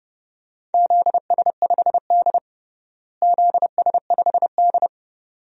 Morse code, heard as a keyed tone, about 32 wpm, 700 Hz. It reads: ZS5D ZS5D
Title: DXCC-100 Call Signs - 4 Characters - Repeated 2x - 30wpm